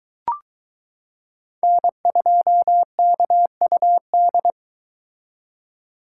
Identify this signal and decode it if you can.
Morse code, heard as a keyed tone, about 22 wpm, 700 Hz. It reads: N2KUD